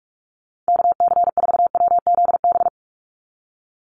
Morse code, heard as a keyed tone, about 44 wpm, 700 Hz. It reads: KC4WZB
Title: NA SOTA Activator Callsigns 45wpm